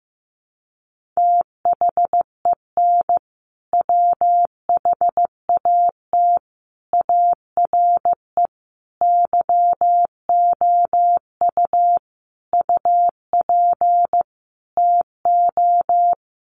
Morse code, heard as a keyed tone, about 15 wpm, 700 Hz. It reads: THEN WHAT ARE YOU UP TO
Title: Sentences from Top 100 Words 15wpm